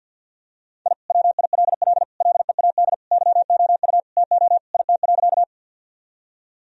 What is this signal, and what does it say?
Morse code, heard as a keyed tone, about 49 wpm, 700 Hz. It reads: I WILL LEAD YOU TO IT.
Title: Sentences from Top 1000 Words - Review Part 2 - 50wpm